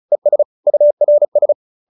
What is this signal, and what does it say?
Morse code, heard as a keyed tone, about 34 wpm, 590 Hz. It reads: ES URS
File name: QSO314 35